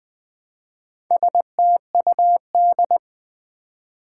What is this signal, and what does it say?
Morse code, heard as a keyed tone, about 20 wpm, 700 Hz. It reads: STUD
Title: Build A Word - 1 Letter At A Time - 3 to 5-Letter Words - 20wpm